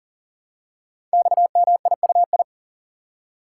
Morse code, heard as a keyed tone, about 39 wpm, 700 Hz. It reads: XMIUI
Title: Code Groups - 5 Letters - 40wpm